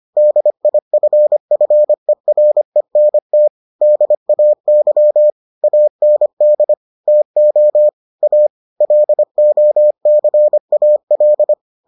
Morse code, heard as a keyed tone, about 25 wpm, 600 Hz. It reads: DIFFERENT DAY AND TO A LOCAL